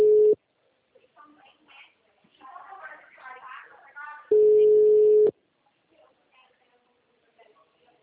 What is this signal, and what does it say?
Ringback tone (Europe, 425 Hz), with speech.